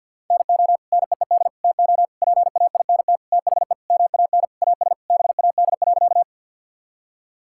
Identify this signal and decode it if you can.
Morse code, heard as a keyed tone, about 50 wpm, 700 Hz. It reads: NO NEED TO PAINT THE MAN AS BAD.
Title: Sentences from Top 1000 Words - Review Part 6 - 50wpm